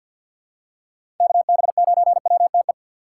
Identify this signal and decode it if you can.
Morse code, heard as a keyed tone, about 51 wpm, 700 Hz. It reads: KB9WTE
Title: USA Call Signs - 6 Characters - 4x Character Spacing - 50wpm